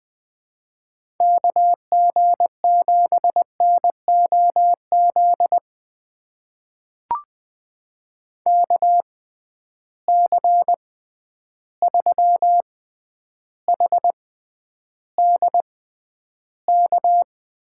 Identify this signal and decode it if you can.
Morse code, heard as a keyed tone, about 21 wpm, spaced as for 10 wpm, 700 Hz. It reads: KG7NOZ KC3HDK
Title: USA Call Signs - 6 Characters - 6x Character Spacing - 20wpm